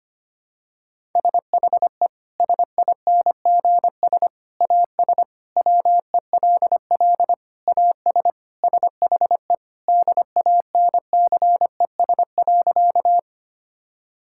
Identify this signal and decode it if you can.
Morse code, heard as a keyed tone, about 25 wpm, 700 Hz. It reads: SHE SINGS AS WELL AS SHE DANCES.
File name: practice / 25wpm / Sentences with 3-Word Phrases - 25wpm